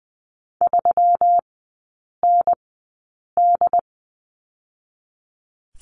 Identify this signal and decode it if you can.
Morse code, heard as a keyed tone, about 20 wpm, spaced as for 8 wpm, 700 Hz. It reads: 3ND
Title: 121 Lesson 20e - Number 3 - Three Characters 20wpm